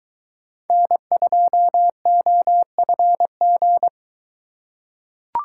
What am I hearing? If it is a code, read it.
Morse code, heard as a keyed tone, about 24 wpm, 700 Hz. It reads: N2OFG